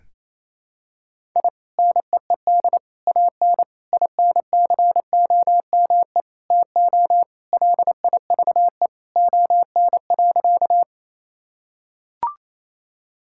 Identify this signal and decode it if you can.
Morse code, heard as a keyed tone, about 28 wpm, 700 Hz. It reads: I NEED AN INCOME TO LIVE ON.
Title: Sentences from Top 1000 Words - Review Part 5 - 28wpm